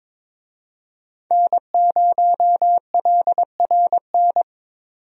Morse code, heard as a keyed tone, about 22 wpm, 700 Hz. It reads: N0LRN
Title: USA Call Signs - 5 Characters - 22wpm